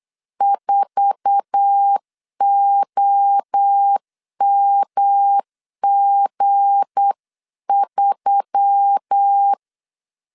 Morse code, heard as a keyed tone, about 8 wpm, 800 Hz. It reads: 4OMG3
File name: LEZIONI / 10 SIGNS / 10D COMA 8wpm 3749